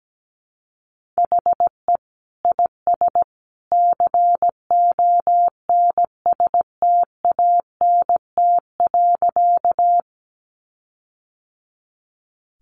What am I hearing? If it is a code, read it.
Morse code, heard as a keyed tone, about 17 wpm, 700 Hz. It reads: HE IS CONSTANT.
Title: Sentences from Top 1500 Words 17wpm